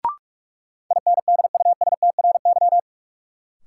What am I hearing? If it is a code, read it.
Morse code, heard as a keyed tone, about 42 wpm, 700 Hz. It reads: INDUSTRY
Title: Sentences from Top 1300 Words - New Word Drill Alt - 45wpm